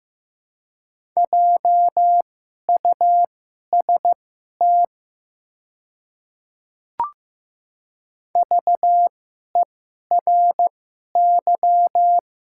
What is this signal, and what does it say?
Morse code, heard as a keyed tone, about 15 wpm, spaced as for 11 wpm, 700 Hz. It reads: JUST VERY
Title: 2x Character Spacing - Top 200 Words - 15wpm